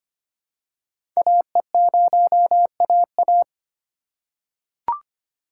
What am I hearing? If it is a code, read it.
Morse code, heard as a keyed tone, about 25 wpm, 700 Hz. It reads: AE0AA
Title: USA Call Signs - 5 Characters - 2x Character Spacing - 25wpm